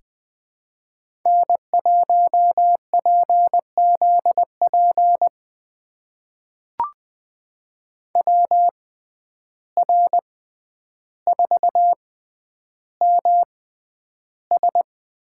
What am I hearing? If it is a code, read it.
Morse code, heard as a keyed tone, about 20 wpm, spaced as for 10 wpm, 700 Hz. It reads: N1PZP WR4MS
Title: USA Call Signs - 5 Characters - 6x Character Spacing - 20wpm